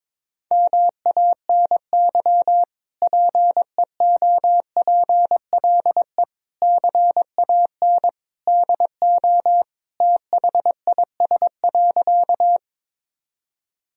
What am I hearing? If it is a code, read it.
Morse code, heard as a keyed tone, about 22 wpm, 700 Hz. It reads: MANY PEOPLE CAN DO THIS.
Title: Sentences from Top 1000 Words - Review Part 1 - 22wpm